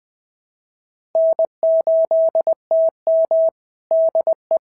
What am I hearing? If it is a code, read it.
Morse code, heard as a keyed tone, about 20 wpm, 650 Hz. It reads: N8TM DE
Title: SOTA QSO Practice 20wpm